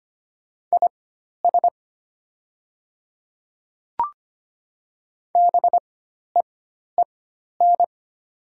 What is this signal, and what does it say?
Morse code, heard as a keyed tone, about 25 wpm, spaced as for 12 wpm, 700 Hz. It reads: IS BEEN
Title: 4x Character Spacing - Top 100 Words - 25wpm